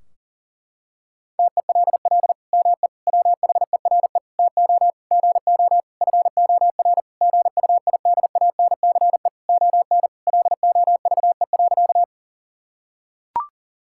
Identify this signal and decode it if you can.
Morse code, heard as a keyed tone, about 40 wpm, 700 Hz. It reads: TELL ME WHERE TO GO FOR GUIDANCE ON LOVE.
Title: Sentences from Top 1600 Words 40wpm